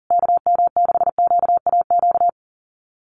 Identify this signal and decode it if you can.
Morse code, heard as a keyed tone, about 40 wpm, 700 Hz. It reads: KM6QAQ
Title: USA Call Signs Repeated 3x 40wpm